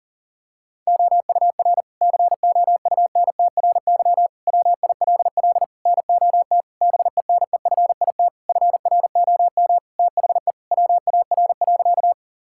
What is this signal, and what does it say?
Morse code, heard as a keyed tone, about 40 wpm, 700 Hz. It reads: OUR COUNTRY WILL NOT BENEFIT FROM THE WAR.
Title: Sentences from Top 1000 Words - Review Part 6 - 40wpm